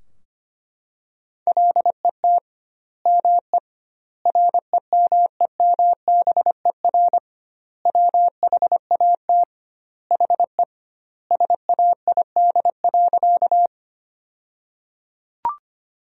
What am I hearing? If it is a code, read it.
Morse code, heard as a keyed tone, about 25 wpm, 700 Hz. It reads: LET ME REMEMBER WHAT HE SAID.
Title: 2x Word-Spacing - Sentences from Top 400 Words - 25wpm